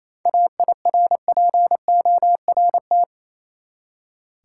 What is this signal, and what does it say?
Morse code, heard as a keyed tone, about 28 wpm, 700 Hz. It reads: AIRPORT
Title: Seven Letter Words - 28wpm